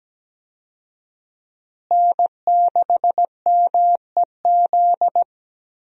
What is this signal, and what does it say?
Morse code, heard as a keyed tone, about 17 wpm, 700 Hz. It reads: N6MEZ
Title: USA Call Signs - 5 Characters - Repeated 2x - 17wpm